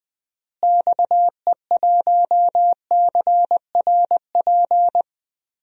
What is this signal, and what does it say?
Morse code, heard as a keyed tone, about 20 wpm, 700 Hz. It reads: XE1CRP
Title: DXCC-100 Call Signs - 6 Characters - Repeated 3x - 20wpm